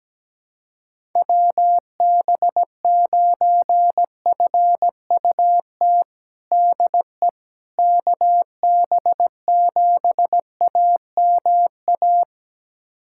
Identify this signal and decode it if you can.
Morse code, heard as a keyed tone, about 17 wpm, 700 Hz. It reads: WB9FUT DE KB7AMA
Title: USA Callsign DE Callsign - 6 Characters - 17wpm